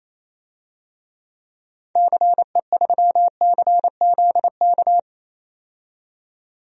Morse code, heard as a keyed tone, about 28 wpm, 700 Hz. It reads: CE3CZK
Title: DXCC-100 Call Signs - 6 Characters - 28wpm